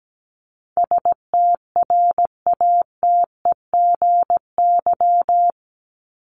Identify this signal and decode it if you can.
Morse code, heard as a keyed tone, about 17 wpm, 700 Hz. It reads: STRATEGY
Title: Top 501-750 Words 17wpm